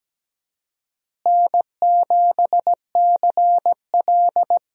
Morse code, heard as a keyed tone, about 17 wpm, 700 Hz. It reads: N7CL
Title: USA Call Signs - 4 Characters - Repeated 3x - 17wpm